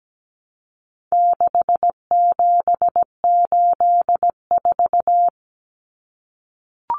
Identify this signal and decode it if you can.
Morse code, heard as a keyed tone, about 17 wpm, 700 Hz. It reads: 6784
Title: Common Cut Numbers - 4-Digit 17wpm